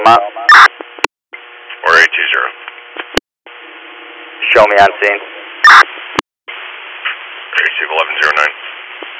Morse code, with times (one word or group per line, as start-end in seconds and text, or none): none